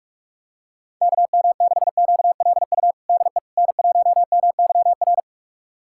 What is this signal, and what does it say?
Morse code, heard as a keyed tone, about 46 wpm, 700 Hz. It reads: KM6QLU DE N1MYR
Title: USA Callsign DE Callsign 45wpm